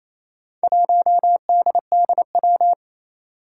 Morse code, heard as a keyed tone, about 28 wpm, 700 Hz. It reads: 1DDW
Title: Four Challenging Letter-Number Combinations 28wpm